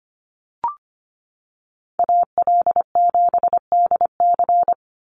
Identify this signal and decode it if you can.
Morse code, heard as a keyed tone, about 25 wpm, 700 Hz. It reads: AL7DC
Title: USA Call Signs 25wpm